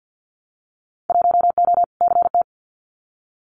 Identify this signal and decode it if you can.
Morse code, heard as a keyed tone, about 50 wpm, 700 Hz. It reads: 1O CT
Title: Winter Field Day Exchanges 50wpm